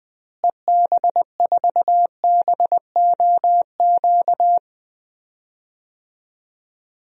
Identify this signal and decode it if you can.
Morse code, heard as a keyed tone, about 20 wpm, 700 Hz. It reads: EB4BOQ
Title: DXCC-100 Call Signs - 6 Characters - Repeated 3x - 20wpm